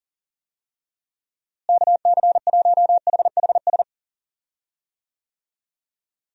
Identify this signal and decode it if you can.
Morse code, heard as a keyed tone, about 40 wpm, 700 Hz. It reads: KC1HHS
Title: USA Call Signs - 6 Characters - Repeated 3x - 40wpm